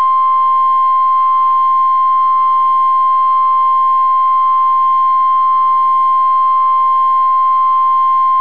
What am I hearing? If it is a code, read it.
Emergency Alert System (SAME) tones (the 1050 Hz attention tone). It is heard from 0.8 to 8.1 s.